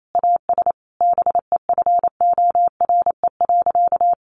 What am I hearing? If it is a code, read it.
Morse code, heard as a keyed tone, about 28 wpm, 700 Hz. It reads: AS BEFORE.
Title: Sentences from Top 1400 Words 28wpm